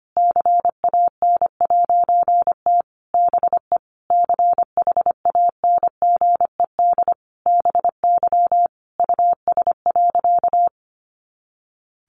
Morse code, heard as a keyed tone, about 25 wpm, 700 Hz. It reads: CAN'T BE CHANGED BY US.
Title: A Sound Of Thunder 25wpm